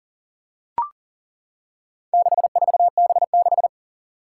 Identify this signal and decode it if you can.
Morse code, heard as a keyed tone, about 40 wpm, 700 Hz. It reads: 64B6
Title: Four Challenging Letter-Number Combinations 40wpm